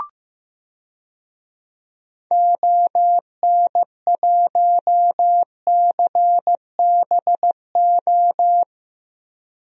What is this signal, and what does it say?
Morse code, heard as a keyed tone, about 15 wpm, 700 Hz. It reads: ON1CBO
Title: DXCC-100 Call Signs - 6 Characters - 15wpm